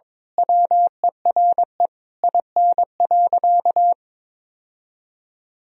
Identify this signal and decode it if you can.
Morse code, heard as a keyed tone, about 22 wpm, 700 Hz. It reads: WERE IN.
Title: Sentences from Top 1000 Words - Review Part 3 - 22wpm